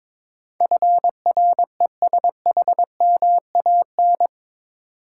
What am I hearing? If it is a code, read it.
Morse code, heard as a keyed tone, about 21 wpm, 700 Hz. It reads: FRESHMAN
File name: practice / 22wpm / Eight Letter Words - 22wpm